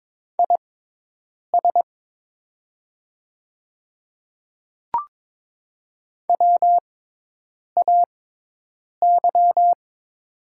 Morse code, heard as a keyed tone, about 21 wpm, spaced as for 8 wpm, 690 Hz. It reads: IS WAY
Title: 6x Character Spacing - Top 100 Words - 22wpm